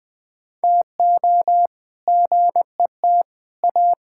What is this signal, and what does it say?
Morse code, heard as a keyed tone, about 20 wpm, 700 Hz. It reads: TO GET A
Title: All-Age Appropriate Jokes - 20wpm